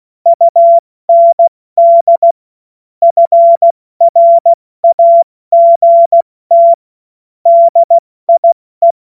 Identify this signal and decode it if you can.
Morse code, heard as a keyed tone, about 16 wpm, 680 Hz. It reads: UND FRAGT DIE
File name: Morsekurs / Witze / Witz 22. August 2023 70 F80